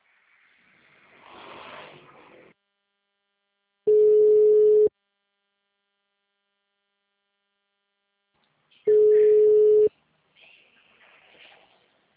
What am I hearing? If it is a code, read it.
Ringback tone (Europe, 425 Hz).